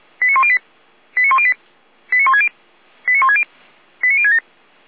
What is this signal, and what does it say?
Five-tone selective call (ZVEI), heard as quick runs of five tones, tones of about 70 ms. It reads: 89108 89108 89160 89160 89067